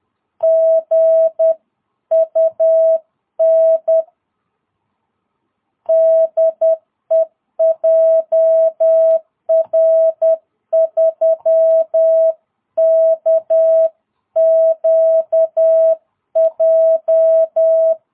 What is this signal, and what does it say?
Morse code, heard as a keyed tone, about 10 wpm, 660 Hz. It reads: GUN DE JR3KQJ
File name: jh8cbh / 2022cw